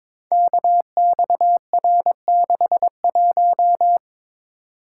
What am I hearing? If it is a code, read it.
Morse code, heard as a keyed tone, about 22 wpm, 700 Hz. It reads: KXR61
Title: Code Groups - 5 Letter:Numbers - 22wpm